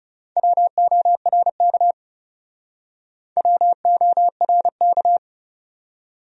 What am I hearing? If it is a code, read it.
Morse code, heard as a keyed tone, about 36 wpm, 700 Hz. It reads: WORK WORK
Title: Speed-Racing Top 300 Words 30wpm